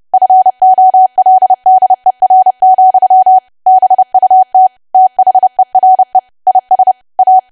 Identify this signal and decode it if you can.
Morse code, heard as a keyed tone, about 30 wpm, 750 Hz. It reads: FOLDER, BUT THERE IS A
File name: cwa / W1AW-Level3 / 30WPM